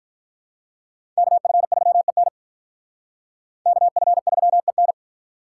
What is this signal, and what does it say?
Morse code, heard as a keyed tone, about 45 wpm, 700 Hz. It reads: KF3EN KF3EN